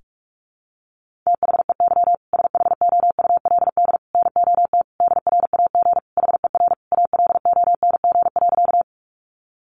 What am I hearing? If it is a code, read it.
Morse code, heard as a keyed tone, about 44 wpm, 700 Hz. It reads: THEY SHOULD NOT DRAG HER ALONG.
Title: Sentences from Top 1500 Words 45wpm